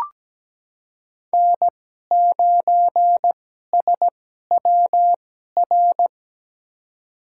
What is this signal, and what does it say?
Morse code, heard as a keyed tone, about 17 wpm, spaced as for 12 wpm, 700 Hz. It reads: N9SWR
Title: USA Call Signs - 5 Characters - 2x Character Spacing - 17wpm